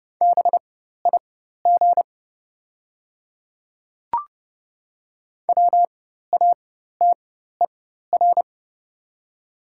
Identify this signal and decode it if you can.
Morse code, heard as a keyed tone, about 30 wpm, spaced as for 14 wpm, 700 Hz. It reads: BIG WATER
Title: 4x Character Spacing - Top 300 Words - 30wpm